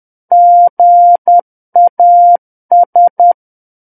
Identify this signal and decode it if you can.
Morse code, heard as a keyed tone, about 10 wpm, 700 Hz. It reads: GAS
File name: Book One / 01-Chapter 1 - 10 WPM